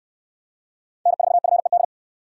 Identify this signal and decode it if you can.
Morse code, heard as a keyed tone, about 69 wpm, 700 Hz. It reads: N4FED